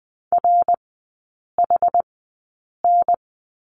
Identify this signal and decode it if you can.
Morse code, heard as a keyed tone, about 20 wpm, spaced as for 8 wpm, 700 Hz. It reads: RHN
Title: 127 Lesson 21e - Number 6 - Three Characters 20wpm